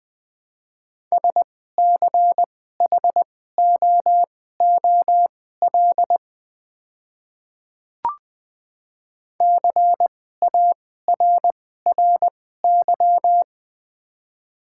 Morse code, heard as a keyed tone, about 20 wpm, spaced as for 15 wpm, 690 Hz. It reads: SCHOOL CARRY
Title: 2x Character Spacing - Top 300 Words - 20wpm